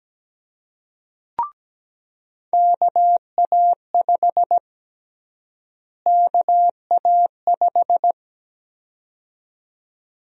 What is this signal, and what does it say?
Morse code, heard as a keyed tone, about 17 wpm, 700 Hz. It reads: KA5 KA5
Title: USA Call Sign Prefixes - Repeated 2x - 17wpm